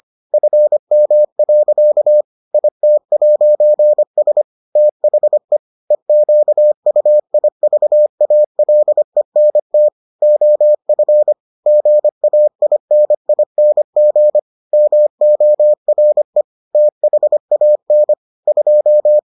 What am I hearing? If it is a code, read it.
Morse code, heard as a keyed tone, about 25 wpm, 600 Hz. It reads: FM. IT'S THE EQUIVALENT OF GAINING MORE THAN 2